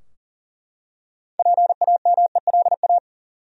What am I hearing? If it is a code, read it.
Morse code, heard as a keyed tone, about 38 wpm, 700 Hz. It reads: PAMELA
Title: USA Common Names - 40wpm